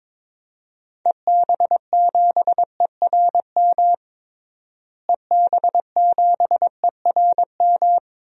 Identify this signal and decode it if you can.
Morse code, heard as a keyed tone, about 22 wpm, 700 Hz. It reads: EB7ERM EB7ERM
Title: DXCC-100 Call Signs - 6 Characters - Repeated 2x - 22wpm